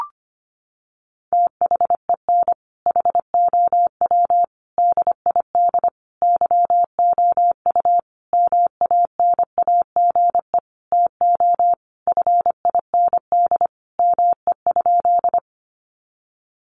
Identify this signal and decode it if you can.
Morse code, heard as a keyed tone, about 25 wpm, 700 Hz. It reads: THEN HOW DID YOU MANAGE TO FIND ME?
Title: Sentences from Top 1400 Words 25wpm